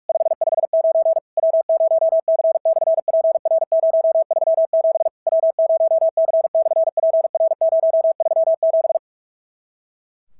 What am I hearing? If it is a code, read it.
Morse code, heard as a keyed tone, about 47 wpm, 650 Hz. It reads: 559 W0C/PR037 W0C/PR037
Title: SOTA QSO Practice 45wpm